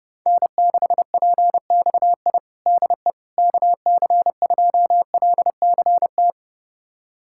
Morse code, heard as a keyed tone, about 30 wpm, 700 Hz. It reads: N6PXI DE KC2LCT
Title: USA Callsign DE Callsign 30wpm